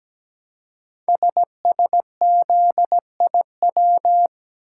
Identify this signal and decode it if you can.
Morse code, heard as a keyed tone, about 17 wpm, 700 Hz. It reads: SSZIW
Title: Code Groups - 5 Letters - 17wpm